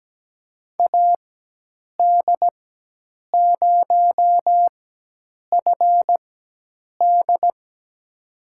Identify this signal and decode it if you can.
Morse code, heard as a keyed tone, about 17 wpm, spaced as for 8 wpm, 700 Hz. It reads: AD0FD
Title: USA Call Signs - 4x Character Spacing - 17wpm